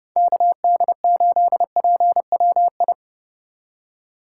Morse code, heard as a keyed tone, about 27 wpm, 700 Hz. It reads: KD8PWI